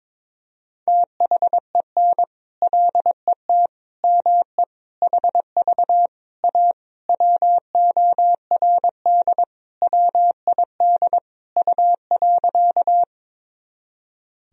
Morse code, heard as a keyed tone, about 22 wpm, 700 Hz. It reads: THEN LET ME HV A WORD WID U.
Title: Sentences from Top 300 Words and Abbreviations 22wpm